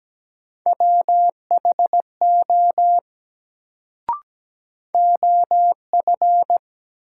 Morse code, heard as a keyed tone, about 17 wpm, 700 Hz. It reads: WHO OF
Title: Sentences from Top 100 Words - New Word Drill - 17wpm